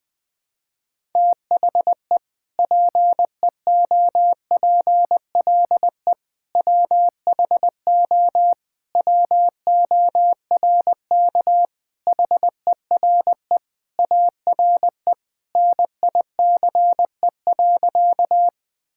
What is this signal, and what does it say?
Morse code, heard as a keyed tone, about 21 wpm, 700 Hz. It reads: THE PEOPLE WHO WORK HERE ARE NICE.
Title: Sentences with 3-Word Phrases - Spoken Beginning - 20wpm